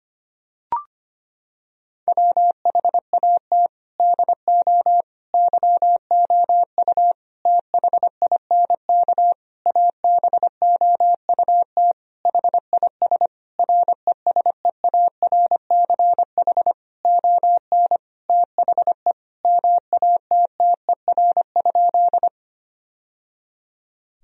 Morse code, heard as a keyed tone, about 25 wpm, 700 Hz. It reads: WHAT DO YOU THINK ABOUT HIS RESEARCH ON THE MATTER?
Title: Sentences from Top 1000 Words - Review Part 4 - 25wpm